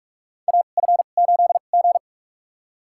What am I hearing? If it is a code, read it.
Morse code, heard as a keyed tone, about 45 wpm, spaced as for 32 wpm, 690 Hz. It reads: AF8G
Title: USA Call Signs - 4 Characters - 2x Character Spacing - 45wpm